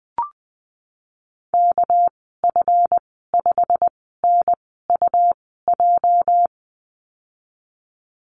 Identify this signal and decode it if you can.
Morse code, heard as a keyed tone, about 20 wpm, spaced as for 15 wpm, 700 Hz. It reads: KF5NUJ
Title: 213 Lesson 34g - Number 8 - Call Signs 20wpm